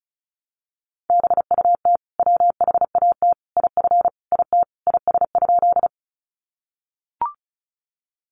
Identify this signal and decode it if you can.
Morse code, heard as a keyed tone, about 34 wpm, 700 Hz. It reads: BUT WHAT IF IT IS?
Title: Sentences from Top 100 Words 35wpm